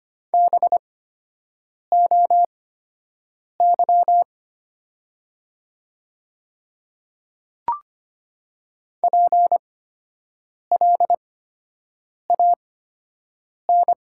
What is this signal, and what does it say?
Morse code, heard as a keyed tone, about 25 wpm, spaced as for 7 wpm, 700 Hz. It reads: BOY PLAN
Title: 8x Character Spacing - Top 500 Words - 25wpm